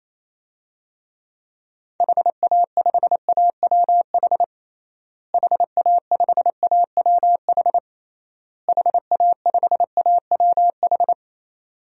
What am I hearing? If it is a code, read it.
Morse code, heard as a keyed tone, about 28 wpm, 700 Hz. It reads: HA5AWH HA5AWH HA5AWH
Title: DXCC-100 Call Signs - 6 Characters - Repeated 3x - 28wpm